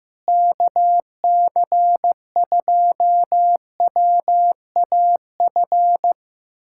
Morse code, heard as a keyed tone, about 15 wpm, 700 Hz. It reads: KC2WAF
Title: USA Callsign DE Callsign 15wpm